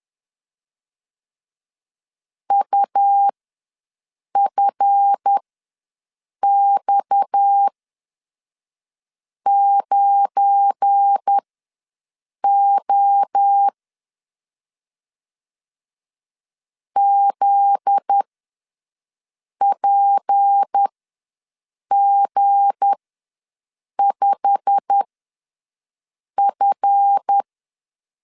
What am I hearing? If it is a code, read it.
Morse code, heard as a keyed tone, about 10 wpm, spaced as for 5 wpm, 800 Hz. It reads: UFX9O ZPG5F
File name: LEZIONI / 8 YJ45 / 8 YJ45 ws 10wpm 3843